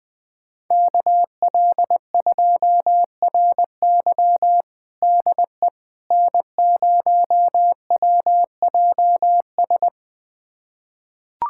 Morse code, heard as a keyed tone, about 20 wpm, 700 Hz. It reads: KL2RY DE N0WJS
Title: USA Callsign DE Callsign - 5 Characters - 20wpm